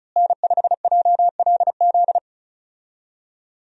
Morse code, heard as a keyed tone, about 36 wpm, 700 Hz. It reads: N5JLZ
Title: USA Call Signs - 5 Characters - Repeated 2x - 35wpm